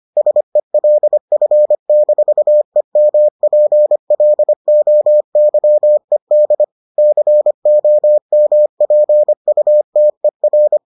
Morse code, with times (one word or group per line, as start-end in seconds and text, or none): 0.2-6.6 SELF-EMPLOYED
7.0-10.8 COMPUTER